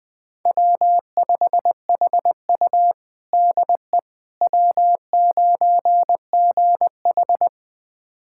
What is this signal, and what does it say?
Morse code, heard as a keyed tone, about 20 wpm, 700 Hz. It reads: W5HU DE W9GH
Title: USA Callsign DE Callsign - 4 Characters - 20wpm